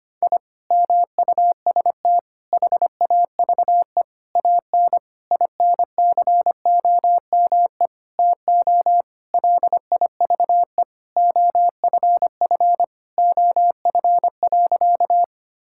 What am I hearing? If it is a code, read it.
Morse code, heard as a keyed tone, about 25 wpm, 700 Hz. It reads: I MUST HAVE AN INCOME TO LIVE OFF OF.
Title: Sentences from Top 1000 Words - Review Part 5 - 25wpm